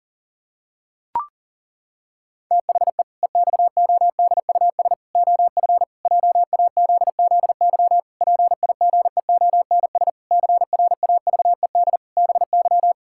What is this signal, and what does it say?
Morse code, heard as a keyed tone, about 37 wpm, 700 Hz. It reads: THE EXODUS OF JAZZY PIGEONS CRAVED BY
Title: Pangram Challenge v2 - 40wpm